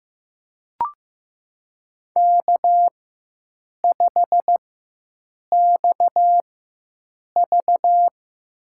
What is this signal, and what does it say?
Morse code, heard as a keyed tone, about 15 wpm, spaced as for 7 wpm, 700 Hz. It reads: K5XV
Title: USA Call Signs - 4 Characters - 4x Character Spacing - 15wpm